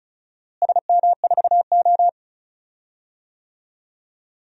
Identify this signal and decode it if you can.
Morse code, heard as a keyed tone, about 36 wpm, 700 Hz. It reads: SM4O
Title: DXCC-100 Call Signs - 4 Characters - Repeated 3x - 35wpm